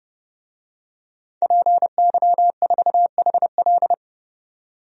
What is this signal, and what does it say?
Morse code, heard as a keyed tone, about 31 wpm, 700 Hz. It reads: PY4HL
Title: DXCC-100 Call Signs - 5 Characters - 30wpm